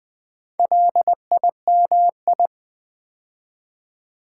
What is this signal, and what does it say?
Morse code, heard as a keyed tone, about 20 wpm, 700 Hz. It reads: LIMI
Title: Build A Word - 4 Letters At A Time - 10-Letter Words - 20wpm